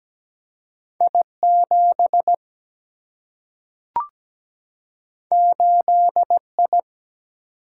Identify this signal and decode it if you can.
Morse code, heard as a keyed tone, about 17 wpm, 700 Hz. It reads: I7 8I
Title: Single-Letter Single-Number 17wpm